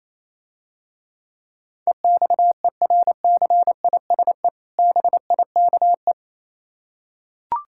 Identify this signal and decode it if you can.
Morse code, heard as a keyed tone, about 27 wpm, 700 Hz. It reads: EXERCISE BIKE
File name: Easy Words - Speed-Racing with Hints - 22wpm